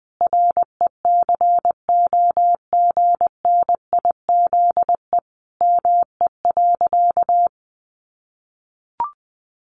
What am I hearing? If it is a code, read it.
Morse code, heard as a keyed tone, about 19 wpm, 700 Hz. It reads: RECOGNIZE ME.
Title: Sentences from Top 1400 Words 20wpm